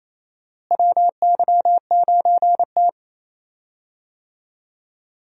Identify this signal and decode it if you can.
Morse code, heard as a keyed tone, about 28 wpm, 700 Hz. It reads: WY9T
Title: USA Call Signs - 4 Characters - Repeated 3x - 28wpm